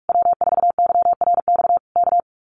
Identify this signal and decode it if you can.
Morse code, heard as a keyed tone, about 46 wpm, 700 Hz. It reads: W4YRX K
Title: Common QSO Phrases Encore 45wpm